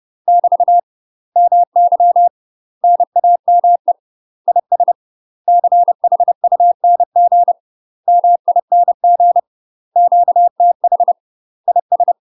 Morse code, heard as a keyed tone, about 30 wpm, 700 Hz. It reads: <BT> MY NAME IS CHUNG MING QTH IS
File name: UK MASTER QSOs TRAINING / 30WPM / UK MASTER QSOs 0004